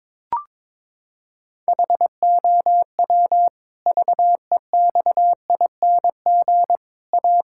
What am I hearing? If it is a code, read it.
Morse code, heard as a keyed tone, about 21 wpm, 700 Hz. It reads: HOW VEXING A
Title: Pangram Challenge v2 - 22wpm